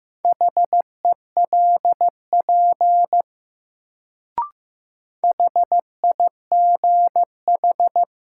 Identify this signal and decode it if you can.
Morse code, heard as a keyed tone, about 15 wpm, 700 Hz. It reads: HELP HIGH
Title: Sentences from Top 200 Words - New Word Drill - 15wpm